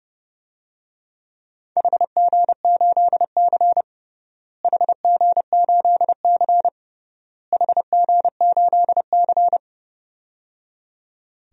Morse code, heard as a keyed tone, about 31 wpm, 700 Hz. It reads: HG8C HG8C HG8C
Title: DXCC-100 Call Signs - 4 Characters - Repeated 3x - 30wpm